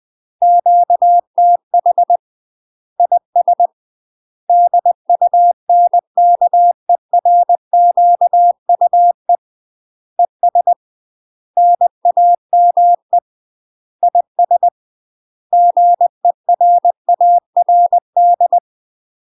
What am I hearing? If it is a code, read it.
Morse code, heard as a keyed tone, about 20 wpm, 700 Hz. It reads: QTH IS DUNKERQUE ES NAME IS GERARD
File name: UK MASTER QSOs TRAINING / 20WPM / UK MASTER QSOs 0004